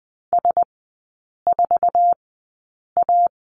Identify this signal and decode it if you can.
Morse code, heard as a keyed tone, about 20 wpm, spaced as for 8 wpm, 700 Hz. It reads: S4A
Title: 074 Lesson 13e - Letter L - Three Characters 20wpm